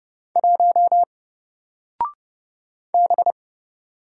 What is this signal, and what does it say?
Morse code, heard as a keyed tone, about 30 wpm, 700 Hz. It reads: 1 B
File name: Single-Letter-Number-30wpm